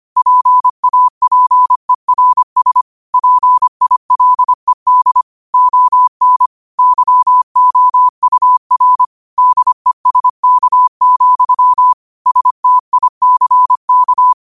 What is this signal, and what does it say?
Morse code, heard as a keyed tone, about 25 wpm, 1000 Hz. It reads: PAPERS PILED ON YOUR DESK, STICK